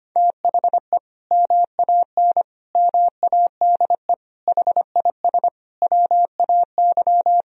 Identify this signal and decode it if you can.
Morse code, heard as a keyed tone, about 25 wpm, 700 Hz. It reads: THE MAN MADE HIS WAY
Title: Sentences from Top 1000 Words - Review Part 2 - 25wpm